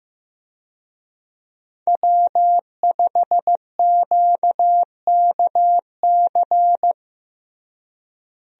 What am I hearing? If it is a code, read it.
Morse code, heard as a keyed tone, about 16 wpm, 700 Hz. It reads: W5QKC
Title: USA Call Signs - 5 Characters - Repeated 3x - 15wpm